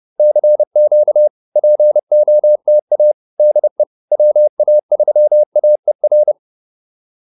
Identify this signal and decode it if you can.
Morse code, heard as a keyed tone, about 30 wpm, 600 Hz. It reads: CQ POTA DE WA3AER
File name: POTA110 30